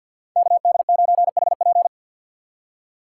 Morse code, heard as a keyed tone, about 52 wpm, 700 Hz. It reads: KD9HP
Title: USA Call Signs - 6x Character Spacing - 50wpm